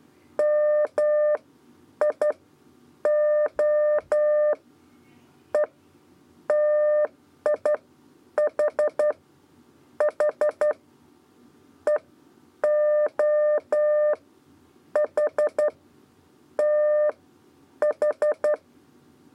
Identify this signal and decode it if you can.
Morse code, heard as a keyed tone, about 11 wpm, spaced as for 9 wpm, 600 Hz. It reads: MI O E TIH H EO H T H
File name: Lezione trasmessa 2 lez 2 exe